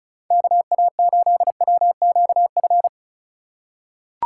Morse code, heard as a keyed tone, about 34 wpm, 700 Hz. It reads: KA8WQF
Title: USA Call Signs - 6 Characters - 2x Character Spacing - 35wpm